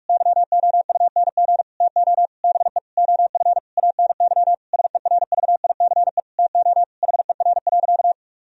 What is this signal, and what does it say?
Morse code, heard as a keyed tone, about 46 wpm, 700 Hz. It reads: YOUNG TO BE OF ANY SERVICE TO HER.
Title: Sentences from Top 1000 Words - Review Part 4 - 45wpm